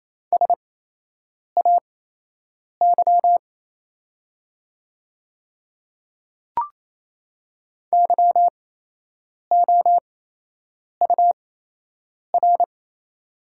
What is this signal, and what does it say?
Morse code, heard as a keyed tone, about 28 wpm, spaced as for 8 wpm, 700 Hz. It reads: SAY YOUR
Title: 8x Character Spacing - Top 200 Words - 28wpm